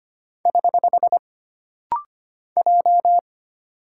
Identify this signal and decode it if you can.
Morse code, heard as a keyed tone, about 25 wpm, 700 Hz. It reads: <HH> J
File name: Single Character (Letter:Number:Punctuation:Prosign) - Rapid-Fire - 25wpm